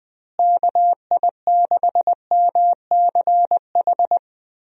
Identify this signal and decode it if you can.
Morse code, heard as a keyed tone, about 20 wpm, 700 Hz. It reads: KI6MCH
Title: USA Call Signs - 6 Characters - Repeated 2x - 20wpm